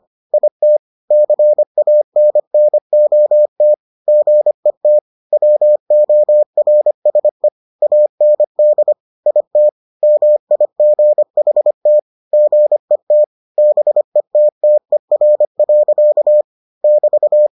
Morse code, heard as a keyed tone, about 25 wpm, 600 Hz. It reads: IT CANNOT GET WORSE AND IT MIGHT GET BETTER. <BT>